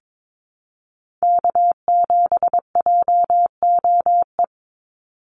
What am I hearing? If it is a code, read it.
Morse code, heard as a keyed tone, about 22 wpm, 700 Hz. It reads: K7JOE
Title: Call Signs Repeated 2x - 22wpm